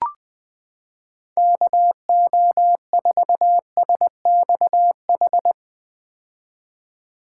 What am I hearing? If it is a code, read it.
Morse code, heard as a keyed tone, about 20 wpm, 700 Hz. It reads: KO4SXH
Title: 2023 POTA Hunter Call Signs - 20wpm